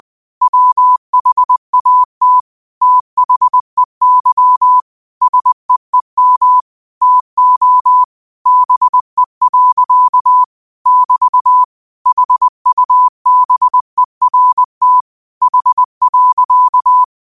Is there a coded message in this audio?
Yes: Morse code from 0.4 to 17.0 s, heard as a keyed tone, about 20 wpm, 1000 Hz. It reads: WHAT THEY SEEM TO BE. <BT> HUBERT H.